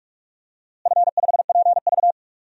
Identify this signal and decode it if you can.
Morse code, heard as a keyed tone, about 45 wpm, 700 Hz. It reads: F5PV
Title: DXCC-100 Call Signs - 4 Characters - Repeated 2x - 45wpm